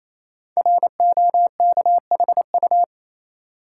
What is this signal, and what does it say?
Morse code, heard as a keyed tone, about 27 wpm, 700 Hz. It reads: ROKHU
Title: Code Groups - 5 Letters - 28wpm